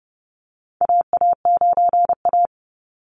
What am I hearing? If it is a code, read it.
Morse code, heard as a keyed tone, about 29 wpm, 700 Hz. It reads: AA9A
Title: Call Signs - 30wpm